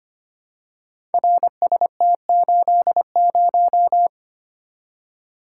Morse code, heard as a keyed tone, about 25 wpm, 700 Hz. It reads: RST80
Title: Code Groups - 5 Letter:Numbers - 25wpm